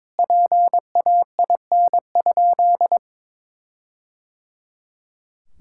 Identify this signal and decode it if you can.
Morse code, heard as a keyed tone, about 22 wpm, 700 Hz. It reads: PAIN?
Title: Sentences from Top 1600 Words 22wpm